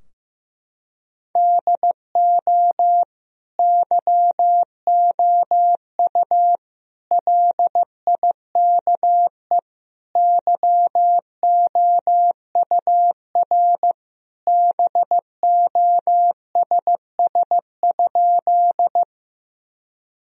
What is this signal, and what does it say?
Morse code, heard as a keyed tone, about 15 wpm, 700 Hz. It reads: DO YOU LIKE YOUR BOSS?
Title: Sentences from Top 1000 Words - Review Part 4 - 15wpm